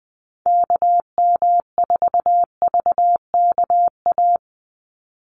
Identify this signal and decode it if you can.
Morse code, heard as a keyed tone, about 20 wpm, 700 Hz. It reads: KM4VKA
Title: USA Call Signs 20wpm